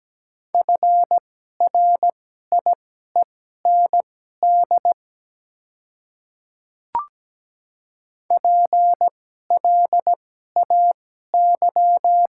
Morse code, heard as a keyed tone, about 17 wpm, spaced as for 12 wpm, 700 Hz. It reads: FRIEND PLAY
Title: 2x Character Spacing - Top 300 Words - 17wpm